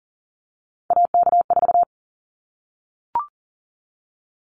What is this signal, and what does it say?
Morse code, heard as a keyed tone, about 40 wpm, 700 Hz. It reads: AK4
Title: Callsign Prefixes for Top 25 DXCC - 40wpm